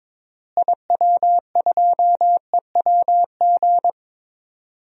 Morse code, heard as a keyed tone, about 21 wpm, 700 Hz. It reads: IW2EWG
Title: DXCC-100 Call Signs - 6 Characters - Repeated 3x - 22wpm